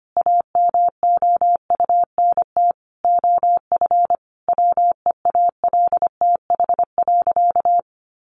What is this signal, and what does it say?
Morse code, heard as a keyed tone, about 25 wpm, 700 Hz. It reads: AMOUNT OF WEALTH.
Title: Sentences from Top 1300 Words 25wpm